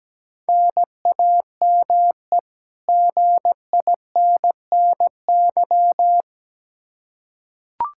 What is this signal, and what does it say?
Morse code, heard as a keyed tone, about 17 wpm, 700 Hz. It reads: NAME GINNY
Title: Practice QSO - Exchange 1 - Beginning - 17wpm